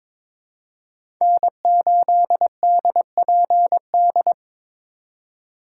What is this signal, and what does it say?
Morse code, heard as a keyed tone, about 22 wpm, 700 Hz. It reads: N8DPD
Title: K1USN SST QSO Practice - 22wpm